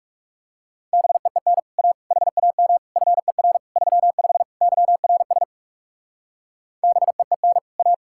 Morse code, heard as a keyed tone, about 37 wpm, 700 Hz. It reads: BEEN A HAM FER 35 YRS BEEN A
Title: Speed-Racing Common QSO Phrases - 30wpm